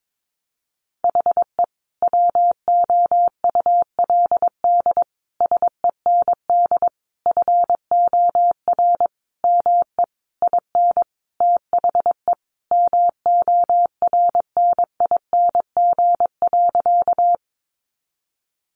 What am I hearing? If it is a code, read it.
Morse code, heard as a keyed tone, about 22 wpm, 700 Hz. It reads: HE WOULD SEND FOR ME IN THE MORNING.
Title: Sentences from Top 1500 Words 22wpm